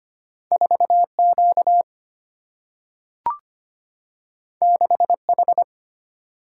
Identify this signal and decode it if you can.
Morse code, heard as a keyed tone, about 25 wpm, 700 Hz. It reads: 4 Q 6 H
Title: Single-Letter Single-Number 25wpm